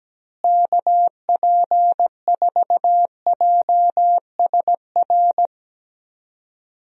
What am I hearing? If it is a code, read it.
Morse code, heard as a keyed tone, about 17 wpm, 700 Hz. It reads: KP4JSR
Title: USA Call Signs - 6 Characters - Repeated 2x - 17wpm